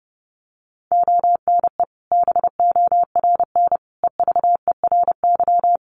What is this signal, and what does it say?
Morse code, heard as a keyed tone, about 30 wpm, 700 Hz. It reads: ONE BORN EVERY
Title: Sentences from Top 1400 Words 30wpm